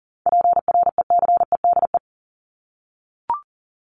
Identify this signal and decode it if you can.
Morse code, heard as a keyed tone, about 40 wpm, 700 Hz. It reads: PRECEDE
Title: ING Suffix - Top 100 Words with Base Words 40wpm